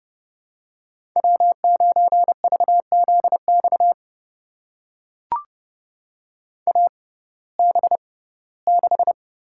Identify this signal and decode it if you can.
Morse code, heard as a keyed tone, about 31 wpm, spaced as for 19 wpm, 700 Hz. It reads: W9VZX AB6
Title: USA Call Signs - 6x Character Spacing - 30wpm